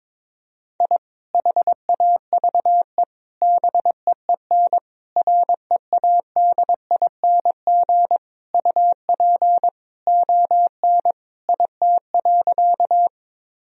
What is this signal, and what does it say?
Morse code, heard as a keyed tone, about 21 wpm, 700 Hz. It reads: I HAVE BEEN READING UP ON IT.
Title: Sentences from Top 1000 Words - Review Part 3 - 22wpm